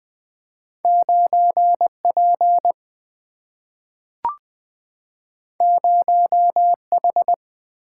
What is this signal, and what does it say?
Morse code, heard as a keyed tone, about 20 wpm, 700 Hz. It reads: 9P 0H
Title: Single-Letter Single-Number 20wpm